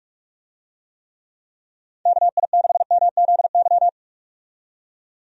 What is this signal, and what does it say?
Morse code, heard as a keyed tone, about 46 wpm, 700 Hz. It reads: KI6MZY
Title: USA Call Signs - 6 Characters - 45wpm